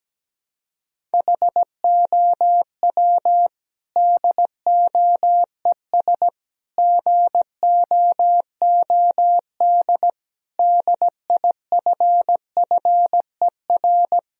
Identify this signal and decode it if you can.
Morse code, heard as a keyed tone, about 17 wpm, 700 Hz. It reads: HOW DOES GOOD DIFFER
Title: Sentences from Top 1000 Words - Review Part 1 - 17wpm